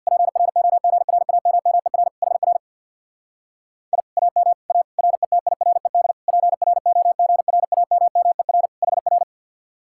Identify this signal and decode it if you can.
Morse code, heard as a keyed tone, about 64 wpm, 700 Hz. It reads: PROGRAMMER HR I AM A RETIRED PROGRAMMER HR